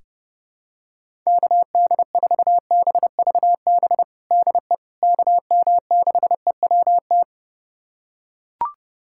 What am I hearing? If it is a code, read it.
Morse code, heard as a keyed tone, about 32 wpm, 700 Hz. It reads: KD4BVB DE KM6EWT
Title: USA Callsign DE Callsign - 6 Characters - 30wpm